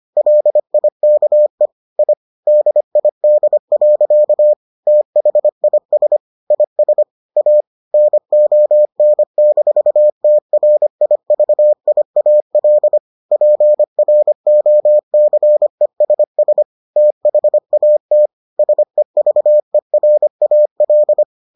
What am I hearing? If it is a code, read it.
Morse code, heard as a keyed tone, about 25 wpm, 600 Hz. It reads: LIKE I DID. THIS IS A NON-TRIVIAL PROCESS THAT SEVERAL